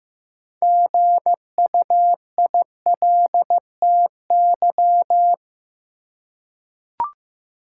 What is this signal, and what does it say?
Morse code, heard as a keyed tone, about 15 wpm, 700 Hz. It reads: GUILTY
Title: Six Letter Words - 15wpm